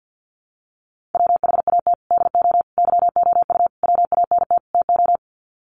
Morse code, heard as a keyed tone, about 48 wpm, 700 Hz. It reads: WHAT DO YOU WANT TO
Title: Sets of 5 Words 50wpm